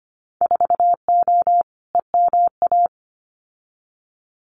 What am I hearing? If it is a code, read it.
Morse code, heard as a keyed tone, about 25 wpm, 700 Hz. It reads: 4O EMA
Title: Winter Field Day Exchanges 25wpm